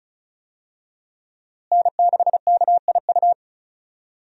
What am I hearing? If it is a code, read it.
Morse code, heard as a keyed tone, about 36 wpm, 700 Hz. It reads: N6KIU
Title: USA Call Signs - 5 Characters - Repeated 3x - 35wpm